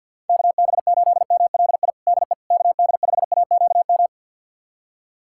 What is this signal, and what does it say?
Morse code, heard as a keyed tone, about 47 wpm, 700 Hz. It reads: KB8MLI DE KD5AQM
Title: USA Callsign DE Callsign 50wpm